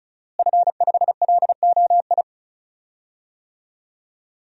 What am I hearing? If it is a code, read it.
Morse code, heard as a keyed tone, about 36 wpm, 700 Hz. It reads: F5LOI
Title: DXCC-100 Call Signs - 5 Characters - Repeated 3x - 35wpm